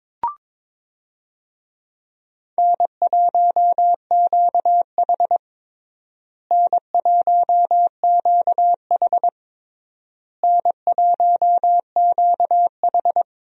Morse code, heard as a keyed tone, about 22 wpm, 700 Hz. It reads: N1QH N1QH N1QH
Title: USA Call Signs - 4 Characters - Repeated 3x - 22wpm